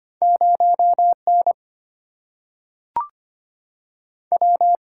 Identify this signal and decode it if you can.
Morse code, heard as a keyed tone, about 25 wpm, 700 Hz. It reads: 0 N W
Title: Single-Letter Single-Number 25wpm